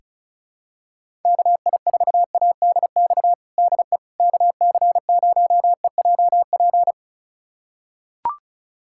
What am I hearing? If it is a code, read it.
Morse code, heard as a keyed tone, about 34 wpm, 700 Hz. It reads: KI4ADX DE KC0EJP
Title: USA Callsign DE Callsign - 6 Characters - 35wpm